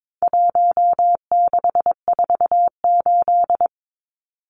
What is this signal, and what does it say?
Morse code, heard as a keyed tone, about 22 wpm, 700 Hz. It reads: 1648
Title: Numbers - 4-Digit 22wpm